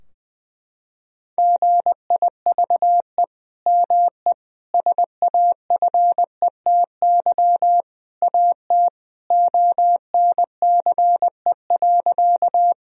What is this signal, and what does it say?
Morse code, heard as a keyed tone, about 21 wpm, 700 Hz. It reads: GIVE ME SAFETY AT ONCE.
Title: Sentences from Top 1000 Words - Review Part 4 - 20wpm